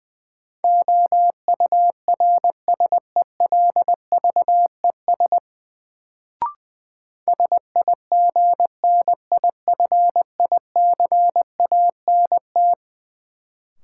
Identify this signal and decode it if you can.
Morse code, heard as a keyed tone, about 20 wpm, 700 Hz. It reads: OURSELVES SIGNIFICANT
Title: Sentences from Top 1600 Words - New Word Drill Alt - 20wpm